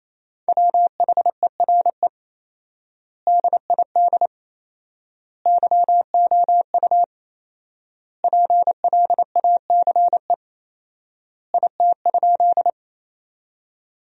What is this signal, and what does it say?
Morse code, heard as a keyed tone, about 27 wpm, 700 Hz. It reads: WHERE DID YOU PLACE IT?
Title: 4x Word-Spacing - Sentences from Top 200 Words - 28wpm